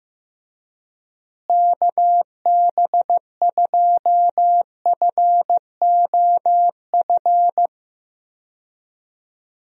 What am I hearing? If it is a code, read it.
Morse code, heard as a keyed tone, about 15 wpm, 700 Hz. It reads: KB2FOF
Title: USA Call Signs - 6 Characters - Repeated 2x - 15wpm